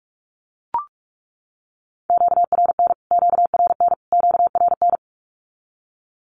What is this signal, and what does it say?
Morse code, heard as a keyed tone, about 44 wpm, 700 Hz. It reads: QRN QRN QRN
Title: Q-Codes Repeated 3x 45wpm